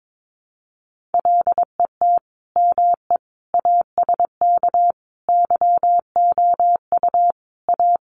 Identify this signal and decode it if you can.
Morse code, heard as a keyed tone, about 22 wpm, 700 Hz. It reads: LET ME ASK YOU A
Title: Sets of 5 Words Encore 22wpm